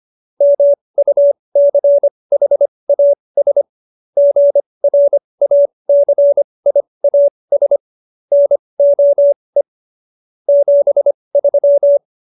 Morse code, heard as a keyed tone, about 25 wpm, spaced as for 20 wpm, 570 Hz. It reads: MUCHAS GRACIAS NOE 73